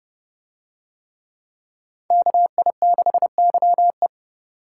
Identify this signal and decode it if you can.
Morse code, heard as a keyed tone, about 29 wpm, 700 Hz. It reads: KI6YE